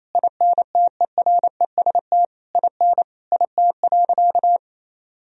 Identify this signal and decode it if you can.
Morse code, heard as a keyed tone, about 28 wpm, 700 Hz. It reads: INTEREST IN IT.
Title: Sentences from Top 1000 Words - Review Part 2 - 28wpm